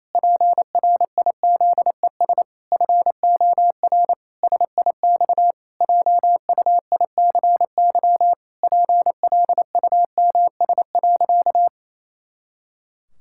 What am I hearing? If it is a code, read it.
Morse code, heard as a keyed tone, about 28 wpm, 700 Hz. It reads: PRIZES FOR SIX JUICY PLUMS.
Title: Pangram Challenge v2 - 28wpm